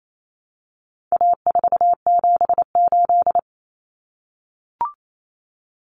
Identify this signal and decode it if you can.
Morse code, heard as a keyed tone, about 28 wpm, 700 Hz. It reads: A478
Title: Common Cut Numbers - 4-Digit 28wpm